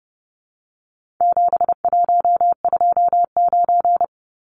Morse code, heard as a keyed tone, about 30 wpm, 700 Hz. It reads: 7129
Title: Numbers - 4-Digit 30wpm